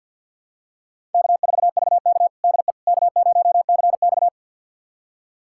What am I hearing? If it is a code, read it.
Morse code, heard as a keyed tone, about 49 wpm, 700 Hz. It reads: K4VK DE K0CX
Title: USA Callsign DE Callsign - 4 Characters - 50wpm